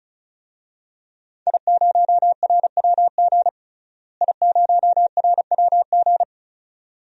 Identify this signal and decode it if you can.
Morse code, heard as a keyed tone, about 36 wpm, 700 Hz. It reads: I0RWG I0RWG
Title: DXCC-100 Call Signs - 5 Characters - Repeated 2x - 35wpm